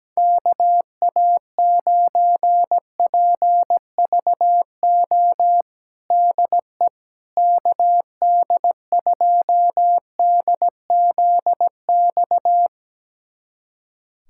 Morse code, heard as a keyed tone, about 17 wpm, 700 Hz. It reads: KA9PVO DE KD2DZX
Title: USA Callsign DE Callsign - 6 Characters - 17wpm